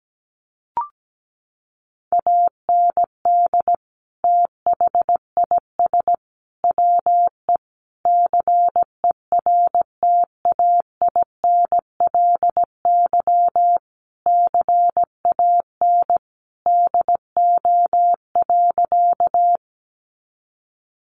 Morse code, heard as a keyed tone, about 17 wpm, 700 Hz. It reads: AND THIS WE CERTAINLY CAN DO.
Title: Sentences from Top 900 Words 17wpm